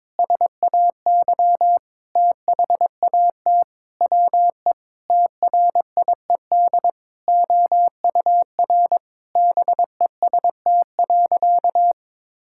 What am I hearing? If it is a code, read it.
Morse code, heard as a keyed tone, about 22 wpm, 700 Hz. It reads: SAY THAT WE TRIED OUR BEST.
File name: Sentences with 7-Word Phrases - 22wpm